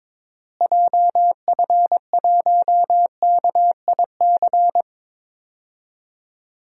Morse code, heard as a keyed tone, about 21 wpm, 700 Hz. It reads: JF1KIC
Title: DXCC-100 Call Signs - 6 Characters - Repeated 3x - 22wpm